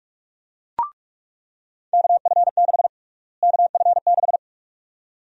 Morse code, heard as a keyed tone, about 42 wpm, 700 Hz. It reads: KF6 KF6
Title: USA Call Sign Prefixes - Repeated 2x - 45wpm